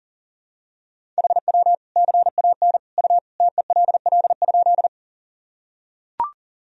Morse code, heard as a keyed tone, about 37 wpm, 700 Hz. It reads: HW CAN U TELL?
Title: Sentences from Top 200 Words and Abbreviations 40wpm